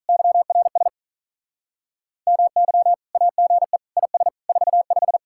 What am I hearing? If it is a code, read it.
Morse code, heard as a keyed tone, about 43 wpm, 700 Hz. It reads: YRS MY AGE IS 45
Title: Speed-Racing Common QSO Phrases - 35wpm